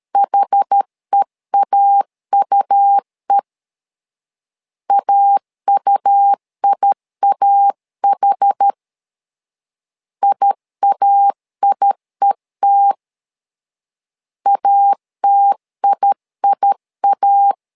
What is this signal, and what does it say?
Morse code, heard as a keyed tone, about 13 wpm, 800 Hz. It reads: HEAUE AUIAH IAIET ATIIA
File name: LEZIONI / 2 UHDN / 2 UH 12wpm 1439